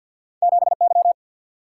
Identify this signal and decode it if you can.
Morse code, heard as a keyed tone, about 51 wpm, 700 Hz. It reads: 7Y
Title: Single-Letter Single-Number 50wpm